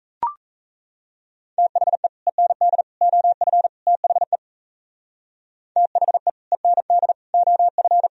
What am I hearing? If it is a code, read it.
Morse code, heard as a keyed tone, about 36 wpm, 700 Hz. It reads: THE END OF THE THE END OF